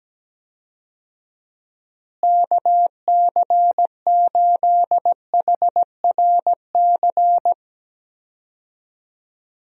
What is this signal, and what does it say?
Morse code, heard as a keyed tone, about 17 wpm, 700 Hz. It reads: KC8HRC
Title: USA Call Signs - 6 Characters - Repeated 3x - 17wpm